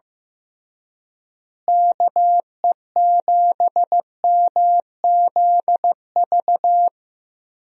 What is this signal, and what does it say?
Morse code, heard as a keyed tone, about 15 wpm, 700 Hz. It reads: KE7MZV
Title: USA Call Signs - 6 Characters - Repeated 2x - 15wpm